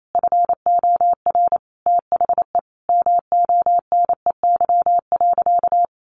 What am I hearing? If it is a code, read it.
Morse code, heard as a keyed tone, about 28 wpm, 700 Hz. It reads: FOR THE MONEY.
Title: Sentences from Top 1300 Words 28wpm